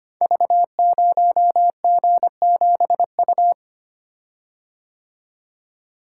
Morse code, heard as a keyed tone, about 25 wpm, 690 Hz. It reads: V0G7U
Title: Code Groups - 5 Letter:Numbers - 25wpm